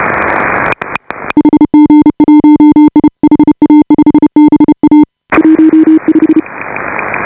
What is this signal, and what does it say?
Morse code, heard as a keyed tone, about 31 wpm, 300 Hz. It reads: HG1I HA5BA 15
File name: AUDIO2018 / HA5BA CW